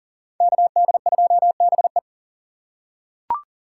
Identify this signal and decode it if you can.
Morse code, heard as a keyed tone, about 40 wpm, 700 Hz. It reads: KD2BE
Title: USA Call Signs - 5 Characters - 4x Character Spacing - 40wpm